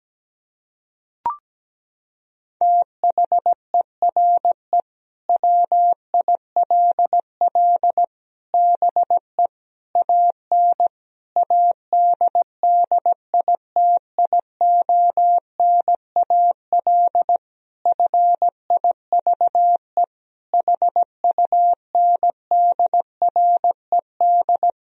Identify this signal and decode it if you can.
Morse code, heard as a keyed tone, about 17 wpm, 700 Hz. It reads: THERE WILL BE AN ADDITIONAL FIVE HUNDRED
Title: Sentences from Top 1000 Words - Review Part 4 - 17wpm